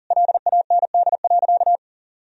Morse code, heard as a keyed tone, about 40 wpm, 700 Hz. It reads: LAND.
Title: Sentences from Top 1000 Words - Review Part 4 - 40wpm